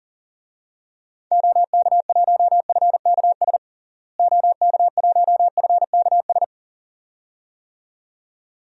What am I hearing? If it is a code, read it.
Morse code, heard as a keyed tone, about 37 wpm, 700 Hz. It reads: OK1FKS OK1FKS
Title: DXCC-100 Call Signs - 6 Characters - Repeated 2x - 40wpm